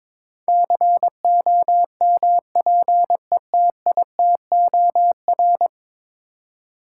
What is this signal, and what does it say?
Morse code, heard as a keyed tone, about 22 wpm, 700 Hz. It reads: COMPETITOR
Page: morse-code-ninja-practice-22wpm